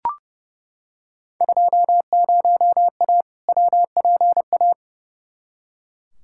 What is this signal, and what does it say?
Morse code, heard as a keyed tone, about 29 wpm, 700 Hz. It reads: 20A WPA
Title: Field Day Exchanges Encore - 30wpm